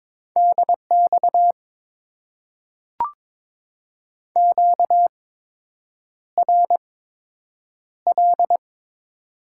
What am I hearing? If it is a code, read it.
Morse code, heard as a keyed tone, about 22 wpm, spaced as for 8 wpm, 690 Hz. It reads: DX QRL
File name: Top 90 QSO Elements - 8x Character Spacing - 22wpm